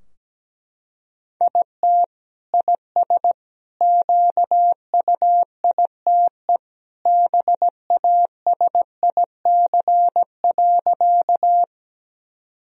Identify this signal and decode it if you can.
Morse code, heard as a keyed tone, about 17 wpm, 700 Hz. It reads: IT IS QUITE BASIC.
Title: Sentences from Top 1000 Words - Review Part 3 - 17wpm